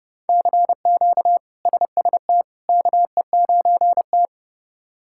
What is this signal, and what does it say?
Morse code, heard as a keyed tone, about 29 wpm, 700 Hz. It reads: CQ SST KE9T
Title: K1USN SST QSO Practice - 30wpm